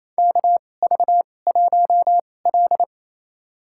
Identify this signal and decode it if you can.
Morse code, heard as a keyed tone, about 28 wpm, spaced as for 20 wpm, 700 Hz. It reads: KV1L
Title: USA Call Signs - 2x Character Spacing - 28wpm